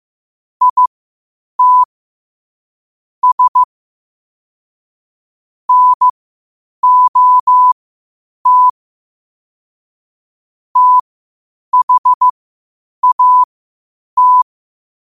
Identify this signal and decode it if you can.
Morse code, heard as a keyed tone, about 15 wpm, spaced as for 8 wpm, 1000 Hz. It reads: IT S NOT THAT
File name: qotdfiles / 8wpm